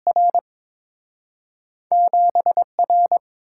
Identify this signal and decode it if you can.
Morse code, heard as a keyed tone, about 30 wpm, spaced as for 26 wpm, 700 Hz. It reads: R 7R
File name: Speed-Racing - Two Letter-Number Combinations - 22wpm